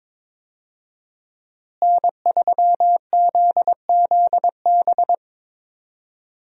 Morse code, heard as a keyed tone, about 22 wpm, 700 Hz. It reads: N3ZZB
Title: USA Call Signs - 5 Characters - Repeated 3x - 22wpm